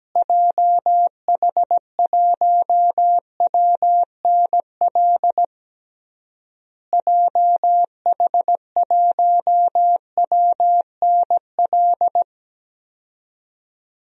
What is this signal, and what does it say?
Morse code, heard as a keyed tone, about 17 wpm, 700 Hz. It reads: JH1WNL JH1WNL
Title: DXCC-100 Call Signs - 6 Characters - Repeated 3x - 17wpm